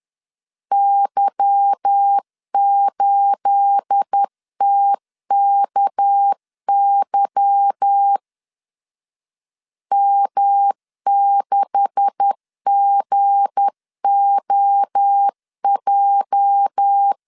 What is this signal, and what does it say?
Morse code, heard as a keyed tone, about 10 wpm, 800 Hz. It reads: Y8TKY M6GOJ
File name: CW / DET / 1 9 1500C S A 10wpm 4511